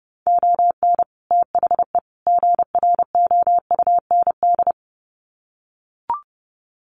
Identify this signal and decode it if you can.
Morse code, heard as a keyed tone, about 30 wpm, 700 Hz. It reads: ON THE GROUND
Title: Sets of 3 Words 30wpm